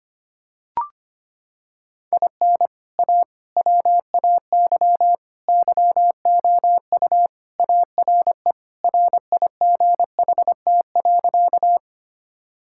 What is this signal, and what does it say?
Morse code, heard as a keyed tone, about 25 wpm, 700 Hz. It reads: IN A WAY YOU ARE RIGHT.
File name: practice / 25wpm / Sentences with 3-Word Phrases - 25wpm